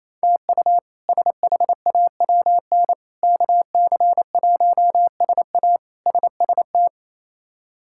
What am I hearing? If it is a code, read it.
Morse code, heard as a keyed tone, about 28 wpm, 700 Hz. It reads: TU SHAWN KC1SA SST
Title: K1USN SST QSO Practice - 28wpm